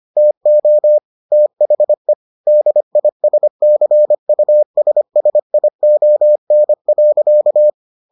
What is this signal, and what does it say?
Morse code, heard as a keyed tone, about 25 wpm, 600 Hz. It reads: TO THE DISCUSSION.